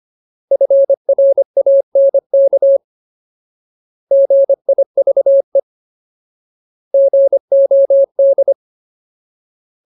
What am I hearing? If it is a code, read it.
Morse code, heard as a keyed tone, about 25 wpm, 550 Hz. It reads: FRANK GIVE GOD
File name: Ultimate Word List / 25WPM Ultimate